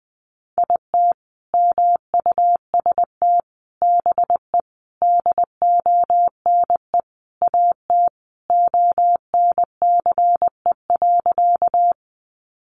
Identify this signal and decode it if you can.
Morse code, heard as a keyed tone, about 20 wpm, 700 Hz. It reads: IT MUST BE DONE AT ONCE.
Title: Sentences from Top 500 Words 20wpm